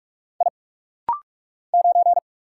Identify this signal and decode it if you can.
Morse code, heard as a keyed tone, about 46 wpm, 700 Hz. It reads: I 9
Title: Single Letter-Number Alt - Rapid-Fire 45wpm